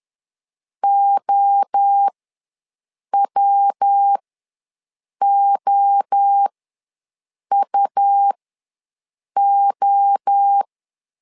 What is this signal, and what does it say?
Morse code, heard as a keyed tone, about 10 wpm, spaced as for 6 wpm, 800 Hz. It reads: OWOUO
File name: LEZIONI / 3 MWRO / 3 MWRO ws 10wpm 2310